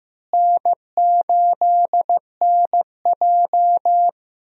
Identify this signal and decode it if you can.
Morse code, heard as a keyed tone, about 15 wpm, 700 Hz. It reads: N8NJ
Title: USA Call Signs - 4 Characters - Repeated 2x - 15wpm